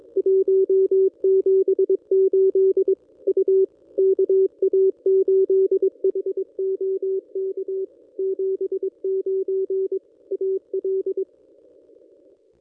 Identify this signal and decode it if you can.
Morse code, heard as a keyed tone, about 22 wpm, 380 Hz. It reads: 178 U KA8HOK 79 AL
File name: contests / ss-cw-k5tr-op-n5rz -1400